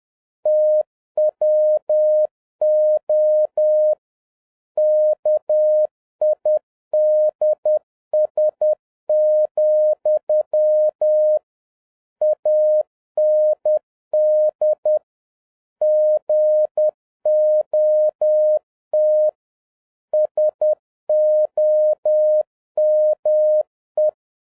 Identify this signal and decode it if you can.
Morse code, heard as a keyed tone, about 10 wpm, 610 Hz. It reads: TWO KIDS, AND GOT SOME